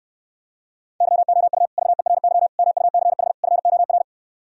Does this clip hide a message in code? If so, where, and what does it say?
Yes: Morse code from 1.0 to 4.0 s, heard as a keyed tone, about 68 wpm, 700 Hz. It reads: YOU VERY MUCH FOR